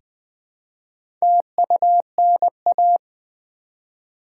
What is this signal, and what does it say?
Morse code, heard as a keyed tone, about 20 wpm, 690 Hz. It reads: TUNA
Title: Easy Words with Hints - 20wpm